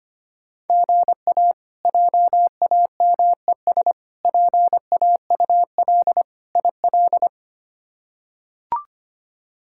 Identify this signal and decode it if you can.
Morse code, heard as a keyed tone, about 25 wpm, 700 Hz. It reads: GA JAMES PAUL IL
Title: K1USN SST QSO Practice - 25wpm